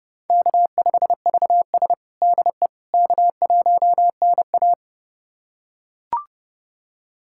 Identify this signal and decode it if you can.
Morse code, heard as a keyed tone, about 32 wpm, 700 Hz. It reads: K5VS DE K1NA
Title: USA Callsign DE Callsign - 4 Characters - 30wpm